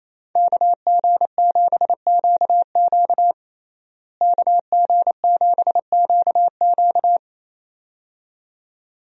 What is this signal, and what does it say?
Morse code, heard as a keyed tone, about 28 wpm, 700 Hz. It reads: KG7QQ KG7QQ
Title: USA Call Signs - 5 Characters - Repeated 2x - 28wpm